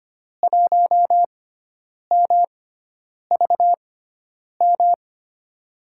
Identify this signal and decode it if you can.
Morse code, heard as a keyed tone, about 25 wpm, spaced as for 9 wpm, 700 Hz. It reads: 1MVM
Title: USA Call Signs - 6x Character Spacing - 25wpm